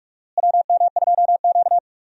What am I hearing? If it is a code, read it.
Morse code, heard as a keyed tone, about 41 wpm, 700 Hz. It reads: WM2Q
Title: USA Call Signs - 4 Characters - Repeated 2x - 45wpm